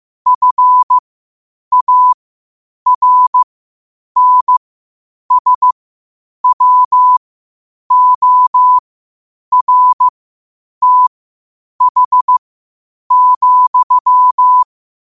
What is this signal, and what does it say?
Morse code, heard as a keyed tone, about 15 wpm, spaced as for 8 wpm, 1000 Hz. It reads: FARNSWORTH,
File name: qotdfiles / 8wpm